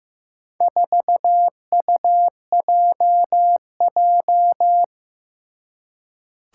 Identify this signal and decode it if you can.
Morse code, heard as a keyed tone, about 15 wpm, 700 Hz. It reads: 4UJJ
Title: Four Challenging Letter-Number Combinations 15wpm